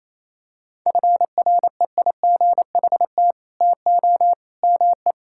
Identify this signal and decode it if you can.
Morse code, heard as a keyed tone, about 28 wpm, 700 Hz. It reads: FREIGHT TO ME
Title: Pangram Challenge v2 - 28wpm